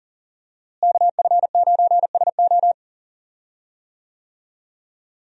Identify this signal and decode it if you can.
Morse code, heard as a keyed tone, about 39 wpm, 700 Hz. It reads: KF9SO
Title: USA Call Signs - 5 Characters - Repeated 3x - 40wpm